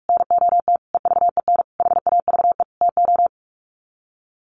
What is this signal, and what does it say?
Morse code, heard as a keyed tone, about 44 wpm, 700 Hz. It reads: NOT EVEN HAVE TO
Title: Sets of 5 Words Encore2 45wpm